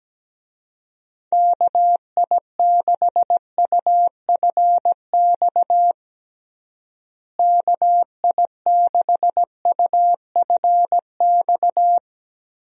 Morse code, heard as a keyed tone, about 17 wpm, 700 Hz. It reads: KI6UFX KI6UFX
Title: USA Call Signs - 6 Characters - Repeated 2x - 17wpm